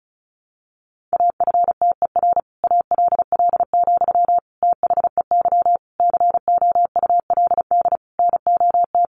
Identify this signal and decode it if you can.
Morse code, heard as a keyed tone, about 34 wpm, 700 Hz. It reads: AFTER ALL, THEY COULD NOT
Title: Sentences from Top 1100 Words 35wpm